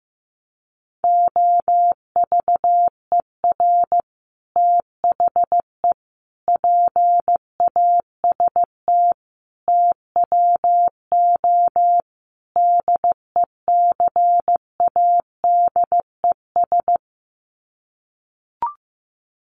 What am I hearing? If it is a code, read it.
Morse code, heard as a keyed tone, about 15 wpm, 700 Hz. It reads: OVER THE PAST TWO DECADES
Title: Sets of 5 Words Encore2 15wpm